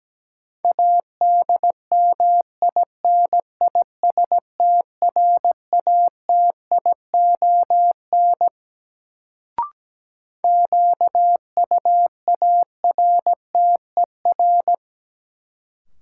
Morse code, heard as a keyed tone, about 17 wpm, 700 Hz. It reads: ADMINISTRATION QUARTER
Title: Sentences from Top 1300 Words - New Word Drill Alt - 17wpm